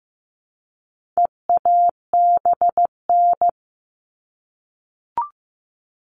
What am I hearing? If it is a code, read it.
Morse code, heard as a keyed tone, about 15 wpm, 700 Hz. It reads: EABN
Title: All Cut Numbers - 4-Digit 15wpm